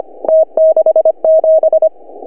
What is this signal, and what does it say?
Morse code, heard as a keyed tone, about 30 wpm, 630 Hz. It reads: T67
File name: CW / HA2OS RHOB 2017 CW